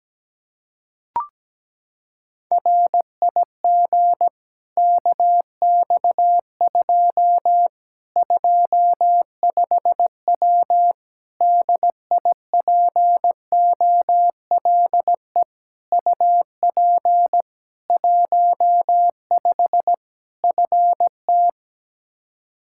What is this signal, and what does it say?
Morse code, heard as a keyed tone, about 17 wpm, 700 Hz. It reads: RIG KX2 25W DIPOLE UP 15 FT
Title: Practice QSO - Exchange 2 - Info - 17wpm